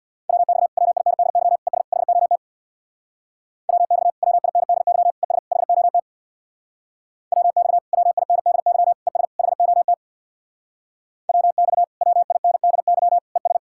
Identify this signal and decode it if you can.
Morse code, heard as a keyed tone, about 64 wpm, 700 Hz. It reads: WX WINDY ES HOT WX WINDY ES HOT WX WINDY ES HOT WX WINDY ES